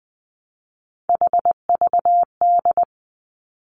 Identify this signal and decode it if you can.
Morse code, heard as a keyed tone, about 19 wpm, 700 Hz. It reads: HVD
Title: Three Challenging Letter-Number Combinations 20wpm